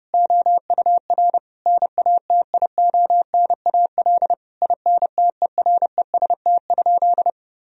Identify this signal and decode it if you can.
Morse code, heard as a keyed tone, about 30 wpm, 700 Hz. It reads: OUR NATIONAL INTEREST?
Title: Sentences from Top 1000 Words - Review Part 4 - 30wpm